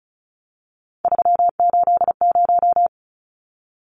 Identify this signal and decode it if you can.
Morse code, heard as a keyed tone, about 34 wpm, 700 Hz. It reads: 380
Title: Numbers - 3-Digit 35wpm